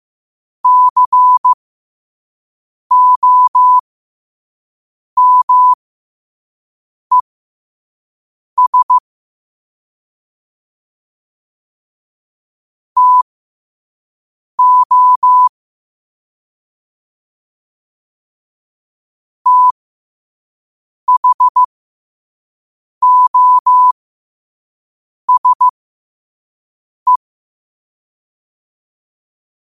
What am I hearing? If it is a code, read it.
Morse code, heard as a keyed tone, about 15 wpm, spaced as for 5 wpm, 1000 Hz. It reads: COMES TO THOSE
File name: qotdfiles / 5wpm